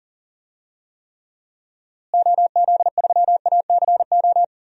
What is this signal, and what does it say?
Morse code, heard as a keyed tone, about 40 wpm, 700 Hz. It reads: OZ3ACO
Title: DXCC-100 Call Signs - 6 Characters - Repeated 3x - 40wpm